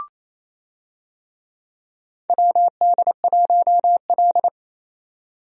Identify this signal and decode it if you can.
Morse code, heard as a keyed tone, about 28 wpm, 700 Hz. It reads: WD1L
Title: USA Call Signs - 4 Characters - 28wpm